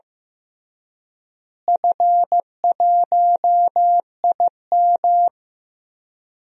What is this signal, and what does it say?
Morse code, heard as a keyed tone, about 15 wpm, 700 Hz. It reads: F1IM
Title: DXCC-100 Call Signs - 4 Characters - Repeated 3x - 15wpm